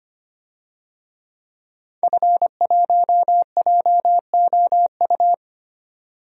Morse code, heard as a keyed tone, about 25 wpm, 700 Hz. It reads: F1JOU
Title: DXCC-100 Call Signs - 5 Characters - 25wpm